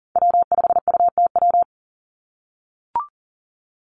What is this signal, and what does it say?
Morse code, heard as a keyed tone, about 40 wpm, 700 Hz. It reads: W5UTW
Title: Call Signs - 40wpm